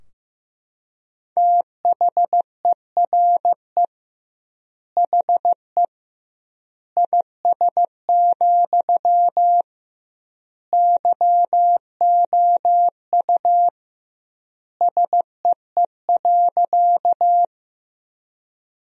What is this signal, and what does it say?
Morse code, heard as a keyed tone, about 15 wpm, 700 Hz. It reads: THERE HE IS, YOU SEE.
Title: 2x Word-Spacing - Sentences from Top 100 Words - 15wpm